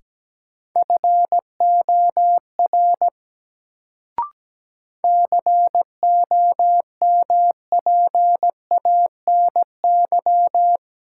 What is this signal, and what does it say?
Morse code, heard as a keyed tone, about 17 wpm, 700 Hz. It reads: FOR COMPANY
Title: Sets of 2 Words - New Word Drill - 17wpm